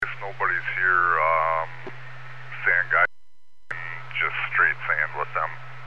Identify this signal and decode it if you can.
Emergency Alert System (SAME) tones (header bursts), heard as harsh buzzing bursts, with speech. It is heard from 0.4 to 5.1 s.